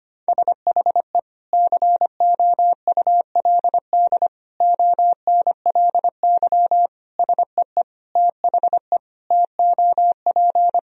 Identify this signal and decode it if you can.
Morse code, heard as a keyed tone, about 25 wpm, 700 Hz. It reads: SHE COULD ONLY SEE THE TOP
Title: Sentences from Top 1000 Words - Review Part 2 - 25wpm